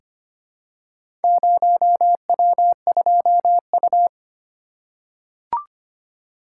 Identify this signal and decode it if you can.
Morse code, heard as a keyed tone, about 25 wpm, 700 Hz. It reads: 0W2U
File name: practice / 25wpm / Four Letter-Number Combinations - 25wpm